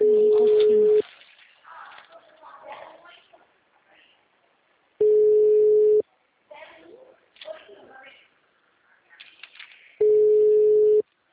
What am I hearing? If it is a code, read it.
Ringback tone (Europe, 425 Hz), with speech.